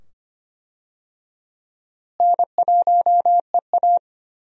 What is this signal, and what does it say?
Morse code, heard as a keyed tone, about 25 wpm, 700 Hz. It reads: N1EA
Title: USA Call Signs - 4 Characters - Repeated 2x - 25wpm